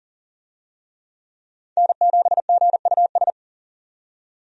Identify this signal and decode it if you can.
Morse code, heard as a keyed tone, about 40 wpm, 700 Hz. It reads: N7GUS
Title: USA Call Signs - 5 Characters - 40wpm